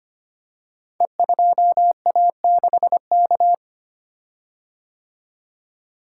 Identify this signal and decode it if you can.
Morse code, heard as a keyed tone, about 25 wpm, 700 Hz. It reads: E2A6K
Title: Code Groups - 5 Letter:Numbers - 25wpm